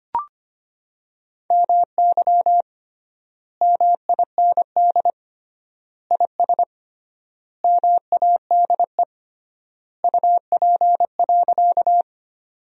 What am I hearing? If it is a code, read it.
Morse code, heard as a keyed tone, about 25 wpm, 700 Hz. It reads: MY MIND IS MADE UP.
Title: 3x Word-Spacing - Sentences from Top 500 Words - 25wpm